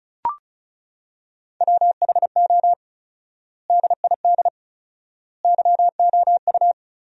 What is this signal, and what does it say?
Morse code, heard as a keyed tone, about 34 wpm, 700 Hz. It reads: WHO DID YOU
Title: 4x Word-Spacing - Sentences from Top 300 Words - 35wpm